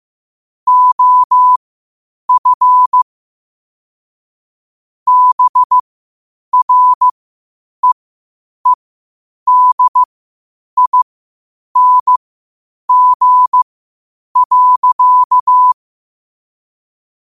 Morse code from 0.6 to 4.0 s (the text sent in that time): OF